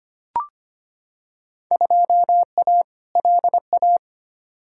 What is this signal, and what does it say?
Morse code, heard as a keyed tone, about 25 wpm, 700 Hz. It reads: 2A LA
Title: Field Day Exchanges - 25wpm